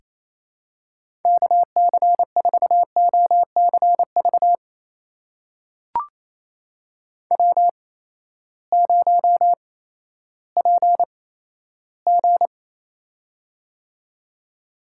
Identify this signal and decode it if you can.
Morse code, heard as a keyed tone, about 28 wpm, spaced as for 14 wpm, 700 Hz. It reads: KC4OCV W0PG
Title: USA Call Signs - 8x Character Spacing - 28wpm